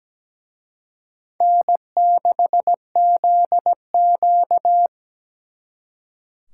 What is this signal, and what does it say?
Morse code, heard as a keyed tone, about 20 wpm, 700 Hz. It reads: N6ZQ
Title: Speed Racing - USA 1x2 Call Signs 17wpm